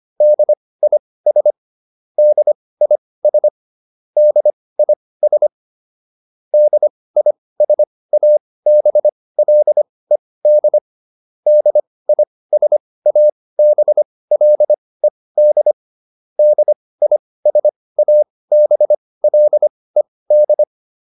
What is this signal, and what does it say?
Morse code, heard as a keyed tone, about 25 wpm, spaced as for 18 wpm, 600 Hz. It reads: DIS DIS DIS DISABLED DISABLED DISABLED
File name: DIS8 18